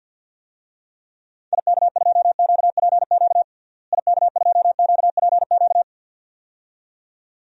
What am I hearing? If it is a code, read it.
Morse code, heard as a keyed tone, about 49 wpm, 700 Hz. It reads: IK2QPQ IK2QPQ
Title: DXCC-100 Call Signs - 6 Characters - Repeated 2x - 50wpm